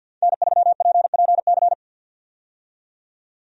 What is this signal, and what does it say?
Morse code, heard as a keyed tone, about 50 wpm, 700 Hz. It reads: N3PPC
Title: USA Call Signs - 5 Characters - 50wpm